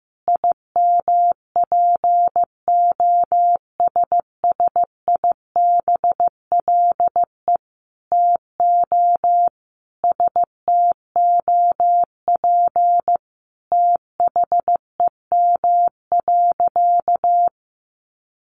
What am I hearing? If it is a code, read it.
Morse code, heard as a keyed tone, about 15 wpm, 700 Hz. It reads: IMPOSSIBLE TO STOP THEM.
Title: Sentences from Top 1100 Words 15wpm